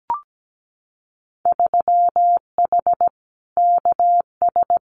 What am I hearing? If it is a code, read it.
Morse code, heard as a keyed tone, about 17 wpm, 700 Hz. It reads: 3H KS
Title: Winter Field Day Exchanges Encore 17wpm